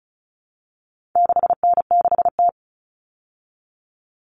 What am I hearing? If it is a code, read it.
Morse code, heard as a keyed tone, about 34 wpm, 700 Hz. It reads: 6N6T
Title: All Cut Numbers - 4-Digit 35wpm